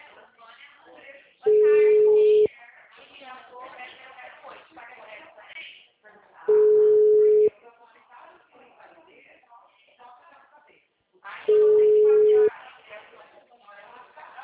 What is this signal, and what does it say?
Ringback tone (Europe, 425 Hz), with speech.